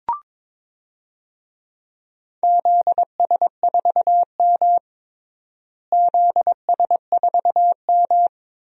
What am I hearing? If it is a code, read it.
Morse code, heard as a keyed tone, about 21 wpm, 700 Hz. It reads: ZS4M ZS4M
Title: DXCC-100 Call Signs - 4 Characters - Repeated 2x - 22wpm